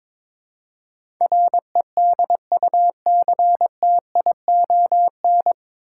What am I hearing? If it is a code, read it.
Morse code, heard as a keyed tone, about 22 wpm, 700 Hz. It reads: REDUCTION
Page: Morse Code Ninja practice 22wpm